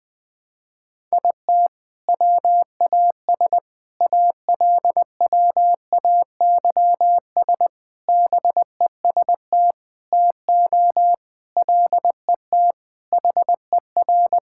Morse code, heard as a keyed tone, about 20 wpm, 700 Hz. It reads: IT WAS ALWAYS BEST TO LET HER
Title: Sentences from Top 1000 Words - Review Part 6 - 20wpm